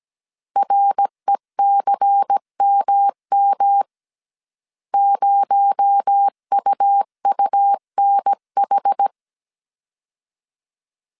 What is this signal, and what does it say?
Morse code, heard as a keyed tone, about 17 wpm, 800 Hz. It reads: RECMM 0UUNH
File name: CW / DET / 1 6 1500C S C 16wpm 2733